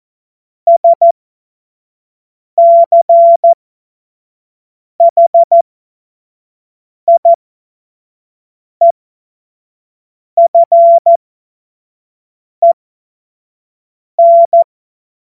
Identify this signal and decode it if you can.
Morse code, heard as a keyed tone, about 14 wpm, spaced as for 5 wpm, 680 Hz. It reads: SCHIEFEN
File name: Morsekurs / Witze / Witz 18. Januar 2022, 25 F70